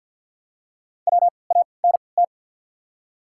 Morse code, heard as a keyed tone, about 53 wpm, spaced as for 29 wpm, 690 Hz. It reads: WANT
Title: 3x Character Spacing - Top 500 Words - 50wpm